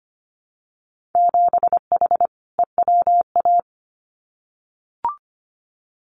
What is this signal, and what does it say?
Morse code, heard as a keyed tone, about 25 wpm, 700 Hz. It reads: 7H EWA
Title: Winter Field Day Exchanges Encore 25wpm